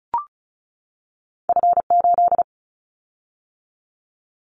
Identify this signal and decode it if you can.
Morse code, heard as a keyed tone, about 34 wpm, 700 Hz. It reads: F8
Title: Callsign Prefixes for Top 25 DXCC - 35wpm